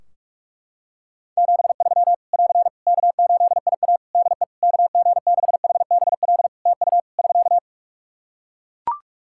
Morse code, heard as a keyed tone, about 41 wpm, 700 Hz. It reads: 73 <AR> K8IA DE KG6HBL TU <SK>
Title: Practice QSO - Exchange 4 - Ending - 45wpm